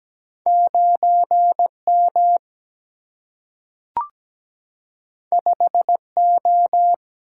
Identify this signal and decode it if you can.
Morse code, heard as a keyed tone, about 17 wpm, 700 Hz. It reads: 9M 5O
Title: Single-Letter Single-Number 17wpm